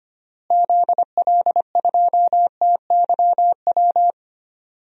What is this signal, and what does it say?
Morse code, heard as a keyed tone, about 25 wpm, 700 Hz. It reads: ZL2TYW
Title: DXCC-100 Call Signs - 6 Characters - Repeated 2x - 25wpm